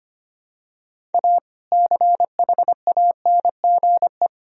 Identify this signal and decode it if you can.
Morse code, heard as a keyed tone, about 25 wpm, 700 Hz. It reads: A CHANGE
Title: Sentences from Top 1000 Words - Review Part 2 - 25wpm